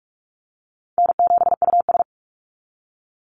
Morse code, heard as a keyed tone, about 44 wpm, 700 Hz. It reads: N7US
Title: Call Signs Repeated 3x - 45wpm